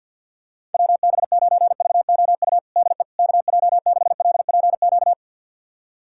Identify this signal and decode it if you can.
Morse code, heard as a keyed tone, about 52 wpm, 700 Hz. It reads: WB9VOU DE KJ6LPQ
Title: USA Callsign DE Callsign - 6 Characters - 50wpm